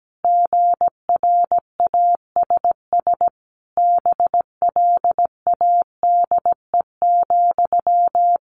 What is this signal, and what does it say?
Morse code, heard as a keyed tone, about 17 wpm, 700 Hz. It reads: GRASS BLADE,
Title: A Sound Of Thunder 17wpm